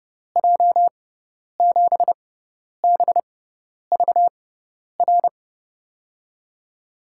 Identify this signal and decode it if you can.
Morse code, heard as a keyed tone, about 30 wpm, spaced as for 10 wpm, 700 Hz. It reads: J7BVR
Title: USA Call Signs - 6 Characters - 6x Character Spacing - 30wpm